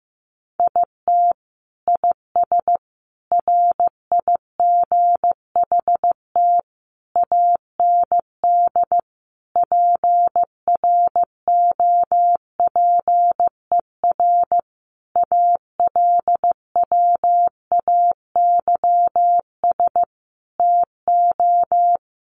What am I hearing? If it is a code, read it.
Morse code, heard as a keyed tone, about 15 wpm, 700 Hz. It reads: IT IS RIGHT AND PROPER ALWAYS TO
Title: Sentences from Top 1500 Words 15wpm